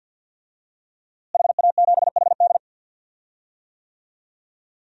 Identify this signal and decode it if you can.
Morse code, heard as a keyed tone, about 50 wpm, 700 Hz. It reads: HA7HD
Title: DXCC-100 Call Signs - 5 Characters - Repeated 3x - 50wpm